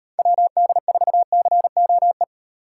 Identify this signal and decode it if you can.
Morse code, heard as a keyed tone, about 38 wpm, 700 Hz. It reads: WD4COE